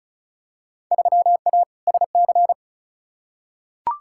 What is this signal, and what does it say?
Morse code, heard as a keyed tone, about 34 wpm, 700 Hz. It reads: 3A SC
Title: Field Day Exchanges - 35wpm